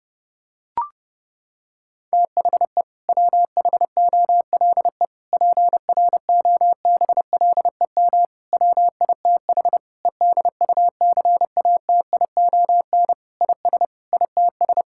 Morse code, heard as a keyed tone, about 30 wpm, 700 Hz. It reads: THE WHOLE PROBLEM WITH EDUCATION IS ITS
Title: Sentences from Top 1000 Words - Review Part 6 - 30wpm